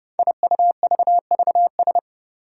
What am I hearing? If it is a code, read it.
Morse code, heard as a keyed tone, about 31 wpm, 700 Hz. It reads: IUVVS
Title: Five Challenging Letter-Number Combinations 30wpm